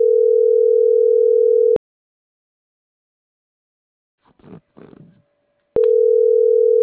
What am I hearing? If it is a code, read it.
Ringback tone (North America), 440 + 480 Hz.